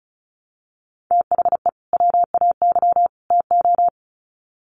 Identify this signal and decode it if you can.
Morse code, heard as a keyed tone, about 34 wpm, 700 Hz. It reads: THE WAY TO
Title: Sets of 3 Words 35wpm